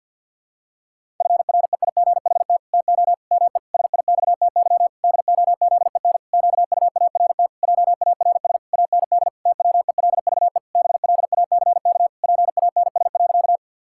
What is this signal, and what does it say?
Morse code, heard as a keyed tone, about 47 wpm, 700 Hz. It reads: FREIGHT TO ME SIXTY DOZEN QUART JARS AND TWELVE BLACK PANS.
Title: Pangram Challenge v2 - 50wpm